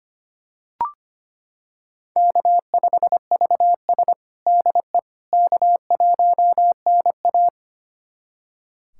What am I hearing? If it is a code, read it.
Morse code, heard as a keyed tone, about 25 wpm, 700 Hz. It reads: K5VS DE K1NA
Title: USA Callsign DE Callsign - 4 Characters - 25wpm